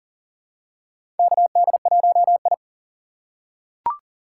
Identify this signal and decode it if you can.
Morse code, heard as a keyed tone, about 41 wpm, 700 Hz. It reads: KD1I
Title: USA Call Signs - 4 Characters - 3x Character Spacing - 40wpm